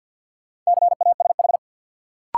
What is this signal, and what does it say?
Morse code, heard as a keyed tone, about 48 wpm, 700 Hz. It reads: CASH
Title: Sentences from Top 700 Words - New Word Drill - 50wpm